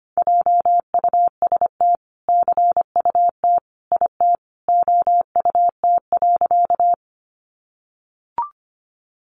Morse code, heard as a keyed tone, about 24 wpm, 700 Hz. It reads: JUST CUT IT OUT.
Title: Sentences from Top 300 Words 25wpm